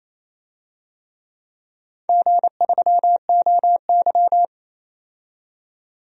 Morse code, heard as a keyed tone, about 28 wpm, 700 Hz. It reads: G3OY
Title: DXCC-100 Call Signs - 4 Characters - 28wpm